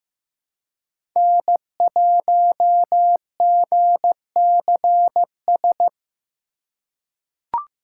Morse code, heard as a keyed tone, about 15 wpm, 700 Hz. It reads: N1GCS
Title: USA Call Signs - 3x Character Spacing - 15wpm